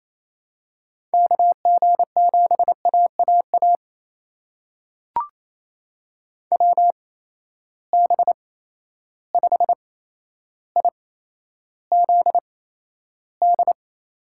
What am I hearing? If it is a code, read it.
Morse code, heard as a keyed tone, about 28 wpm, spaced as for 12 wpm, 700 Hz. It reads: KG7AAA WB5IZD
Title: USA Call Signs - 6 Characters - 8x Character Spacing - 28wpm